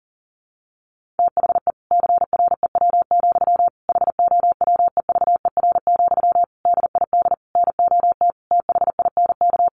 Morse code, heard as a keyed tone, about 40 wpm, 700 Hz. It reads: THE CREW, HOWEVER, DID NOT THINK
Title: Sentences from Top 1500 Words 40wpm